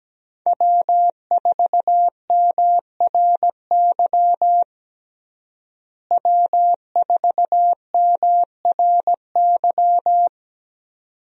Morse code, heard as a keyed tone, about 17 wpm, 700 Hz. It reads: W4MRY W4MRY
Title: USA Call Signs - 5 Characters - Repeated 2x - 17wpm